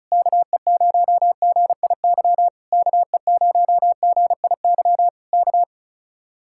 Morse code, heard as a keyed tone, about 33 wpm, 700 Hz. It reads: KE0GIY KE0GIY K
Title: Speed-Racing Common QSO Phrases Encore - 35wpm